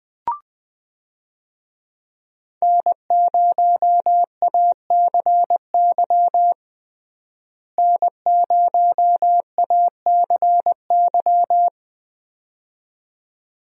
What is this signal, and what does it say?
Morse code, heard as a keyed tone, about 20 wpm, 700 Hz. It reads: N0ACY N0ACY
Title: USA Call Signs - 5 Characters - Repeated 2x - 20wpm